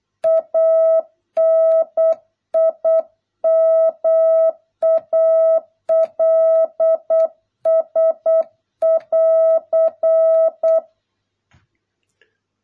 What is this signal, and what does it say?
Morse code, heard as a keyed tone, about 8 wpm, 650 Hz. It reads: ANIMALS <AR>